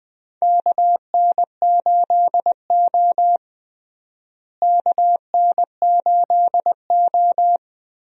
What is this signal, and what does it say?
Morse code, heard as a keyed tone, about 20 wpm, 700 Hz. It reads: KN8O KN8O
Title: USA Call Signs - 4 Characters - Repeated 2x - 20wpm